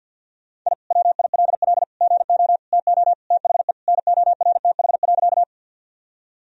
Morse code, heard as a keyed tone, about 46 wpm, 700 Hz. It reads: I WILL GO TO THE NORTH.
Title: Sentences from Top 1000 Words - Review Part 1 - 50wpm